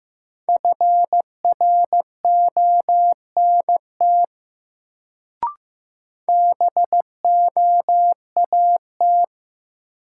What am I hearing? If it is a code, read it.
Morse code, heard as a keyed tone, about 15 wpm, 700 Hz. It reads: FRONT BOAT
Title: Sentences from Top 500 Words - New Word Drill - 15wpm